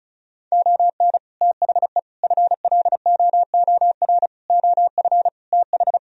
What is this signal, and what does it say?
Morse code, heard as a keyed tone, about 42 wpm, 700 Hz. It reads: ON THE FLOOR OF TH
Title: Speed-Racing - Sets of 5 Words 35wpm